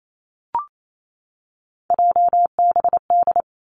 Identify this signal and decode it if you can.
Morse code, heard as a keyed tone, about 28 wpm, 700 Hz. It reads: JBD
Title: Three Challenging Letter-Number Combinations 28wpm